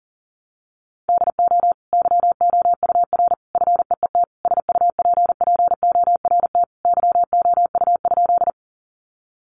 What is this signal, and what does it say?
Morse code, heard as a keyed tone, about 40 wpm, 700 Hz. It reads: DO YOUR FEET SUPPORT YOU?
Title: Sentences from Top 800 Words 40wpm